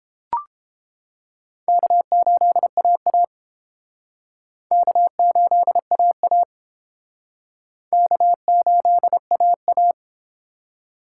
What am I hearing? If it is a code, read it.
Morse code, heard as a keyed tone, about 31 wpm, 700 Hz. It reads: K8AA K8AA K8AA